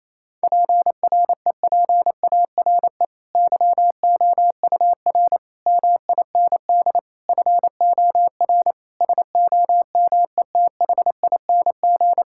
Morse code, heard as a keyed tone, about 28 wpm, 700 Hz. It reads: PREPARE YOUR MIND FOR SOMETHING
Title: Sentences from Top 1000 Words - Review Part 6 - 28wpm